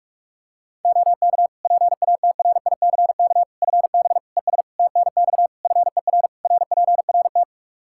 Morse code, heard as a keyed tone, about 42 wpm, 700 Hz. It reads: OK PATRICK FB ES TNX FER RPRT
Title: Practice QSO - Exchange 2 - Info - 45wpm